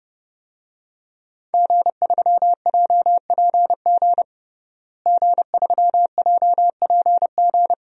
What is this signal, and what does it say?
Morse code, heard as a keyed tone, about 32 wpm, 700 Hz. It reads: G3JPG G3JPG
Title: DXCC-100 Call Signs - 5 Characters - Repeated 2x - 30wpm